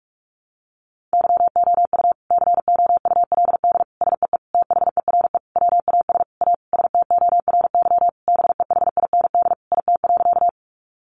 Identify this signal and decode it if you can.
Morse code, heard as a keyed tone, about 44 wpm, 700 Hz. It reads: YOU COULD SEE THERE WAS A STORY BEHIND IT.
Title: Sentences from Top 500 Words 45wpm